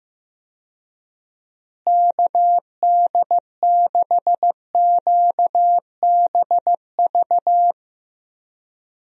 Morse code, heard as a keyed tone, about 15 wpm, 700 Hz. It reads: KD6QBV
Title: USA Call Signs - 6 Characters - Repeated 2x - 15wpm